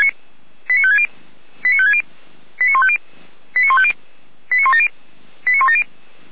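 Five-tone selective call (ZVEI), heard as quick runs of five tones, tones of about 70 ms. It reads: 89560 89560 89150 89150 89170 89170
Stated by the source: ZVEI five-tone selective call received with an SDR, from the audio server of the Orlamünde fire brigade (feuerwehr-orlamuende).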